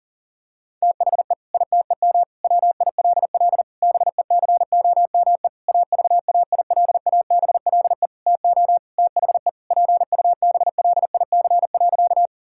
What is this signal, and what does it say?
Morse code, heard as a keyed tone, about 37 wpm, 700 Hz. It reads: THE ITEM WILL BECOME AVAILABLE TO THE PUBLIC.
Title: Sentences from Top 1000 Words - Review Part 6 - 40wpm